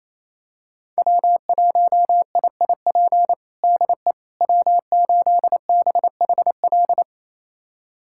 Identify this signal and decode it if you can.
Morse code, heard as a keyed tone, about 28 wpm, 700 Hz. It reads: W1IIP DE W8BHL
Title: USA Callsign DE Callsign - 5 Characters - 28wpm